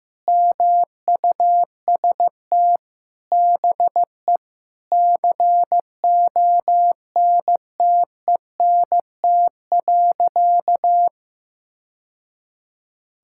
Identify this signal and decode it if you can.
Morse code, heard as a keyed tone, about 15 wpm, 700 Hz. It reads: MUST BE CONTENT.
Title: Sentences from Top 1000 Words - Review Part 6 - 15wpm